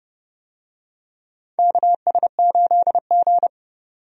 Morse code, heard as a keyed tone, about 30 wpm, 700 Hz. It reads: KS8G
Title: USA Call Signs - 4 Characters - 30wpm